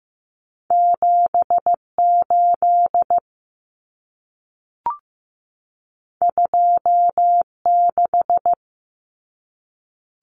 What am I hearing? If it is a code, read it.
Morse code, heard as a keyed tone, about 15 wpm, 700 Hz. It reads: 78 26
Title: Common Cut Numbers - 2-Digit 15wpm